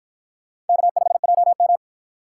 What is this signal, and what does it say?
Morse code, heard as a keyed tone, about 51 wpm, 700 Hz. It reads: K5JM